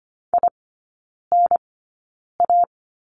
Morse code, heard as a keyed tone, about 25 wpm, spaced as for 9 wpm, 700 Hz. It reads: INA
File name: course / 25:10wpm / 100 Lesson 17e - Letter u - Three Characters 25wpm